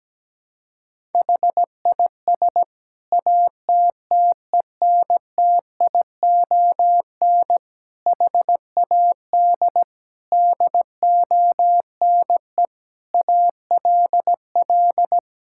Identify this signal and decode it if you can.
Morse code, heard as a keyed tone, about 17 wpm, 700 Hz. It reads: HIS ATTENTION HAD DONE ALL
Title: Sentences from Top 1000 Words - Review Part 6 - 17wpm